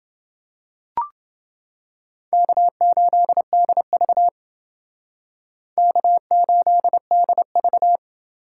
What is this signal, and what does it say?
Morse code, heard as a keyed tone, about 29 wpm, 700 Hz. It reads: K8DV K8DV